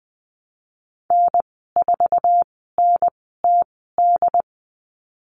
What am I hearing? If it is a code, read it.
Morse code, heard as a keyed tone, about 20 wpm, spaced as for 15 wpm, 700 Hz. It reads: N4NTD
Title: 088 Lesson 15g - Number 5 - Call Signs 20wpm